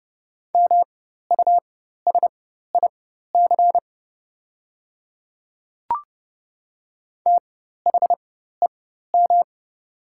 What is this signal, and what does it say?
Morse code, heard as a keyed tone, about 30 wpm, spaced as for 14 wpm, 700 Hz. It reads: MUSIC THEM
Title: 4x Character Spacing - Top 300 Words - 30wpm